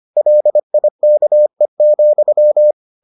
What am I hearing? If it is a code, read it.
Morse code, heard as a keyed tone, about 25 wpm, 600 Hz. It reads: LIKE,